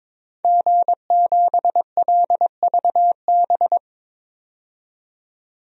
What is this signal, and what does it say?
Morse code, heard as a keyed tone, about 22 wpm, 700 Hz. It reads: G7LVB
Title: DXCC-100 Call Signs - 5 Characters - 22wpm